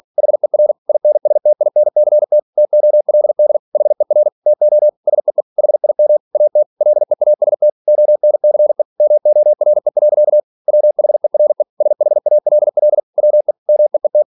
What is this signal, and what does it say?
Morse code, heard as a keyed tone, about 46 wpm, 600 Hz. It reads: HER INSTINCT TOLD HER TO SEE HIM AT LEAST ONCE MORE. WHERE SHALL WE MEET
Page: Never 47wpm